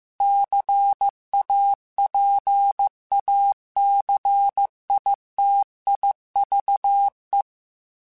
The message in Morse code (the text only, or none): CAPACITIVE